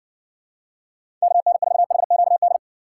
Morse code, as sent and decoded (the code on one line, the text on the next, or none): -.- -. ....- .... --.- -..
KN4HQD